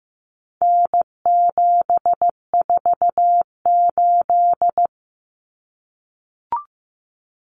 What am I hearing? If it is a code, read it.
Morse code, heard as a keyed tone, about 15 wpm, 700 Hz. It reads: N748
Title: Common Cut Numbers - 4-Digit 15wpm